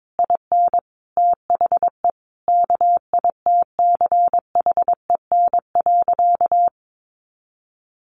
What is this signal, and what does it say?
Morse code, heard as a keyed tone, about 22 wpm, 700 Hz. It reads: IN THE KITCHEN.
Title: Sentences from Top 1400 Words 22wpm